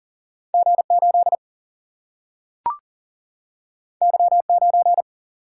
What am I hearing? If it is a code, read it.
Morse code, heard as a keyed tone, about 40 wpm, 700 Hz. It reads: G 8 Y 9
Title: Single-Letter Single-Number 40wpm